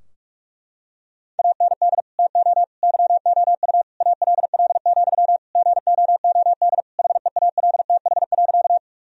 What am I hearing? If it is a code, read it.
Morse code, heard as a keyed tone, about 46 wpm, 700 Hz. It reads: AND TO YOU ALL, GOOD HEALTH.
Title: Sentences from Top 1000 Words - Review Part 3 - 45wpm